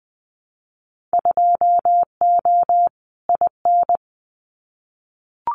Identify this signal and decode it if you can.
Morse code, heard as a keyed tone, about 20 wpm, 700 Hz. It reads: 2O IN
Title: Winter Field Day Exchanges 20wpm